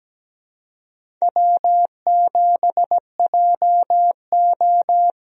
Morse code, heard as a keyed tone, about 17 wpm, 700 Hz. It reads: W7JO
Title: USA Call Signs - 4 Characters - Repeated 2x - 17wpm